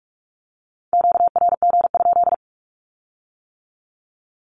Morse code, heard as a keyed tone, about 46 wpm, 700 Hz. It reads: QRG?
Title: Q-Codes with Spoken Meaning - 45wpm